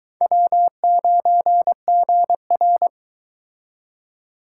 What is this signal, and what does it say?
Morse code, heard as a keyed tone, about 22 wpm, 700 Hz. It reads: W9GR